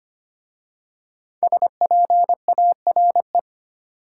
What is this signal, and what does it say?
Morse code, heard as a keyed tone, about 25 wpm, 700 Hz. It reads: SPARE
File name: 5 Letter Words - First 3 Letters Given - 25wpm